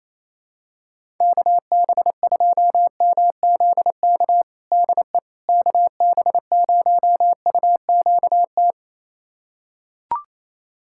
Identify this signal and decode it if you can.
Morse code, heard as a keyed tone, about 28 wpm, 700 Hz. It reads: KB2MZK DE KB0UQT
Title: USA Callsign DE Callsign 28wpm